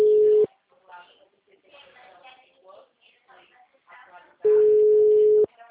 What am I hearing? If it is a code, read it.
Ringback tone (Europe, 425 Hz), with speech.
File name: out- -2017- - - .177